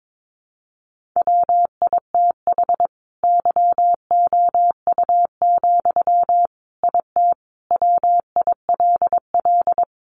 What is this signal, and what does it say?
Morse code, heard as a keyed tone, about 22 wpm, 700 Hz. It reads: WITH YOU, IT WILL
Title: Sentences from Top 1400 Words 22wpm